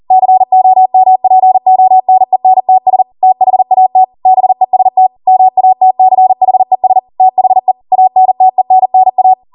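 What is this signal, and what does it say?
Morse code, heard as a keyed tone, about 40 wpm, 750 Hz. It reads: COMPONENTS THAT BEST MATCHES THE ANTENNA
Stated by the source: Code practice at 40 wpm.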